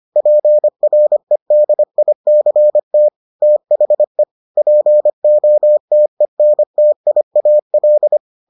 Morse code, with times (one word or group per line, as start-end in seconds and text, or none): 0.2-3.1 PREDICT
3.4-4.2 THE
4.6-8.2 POTENTIAL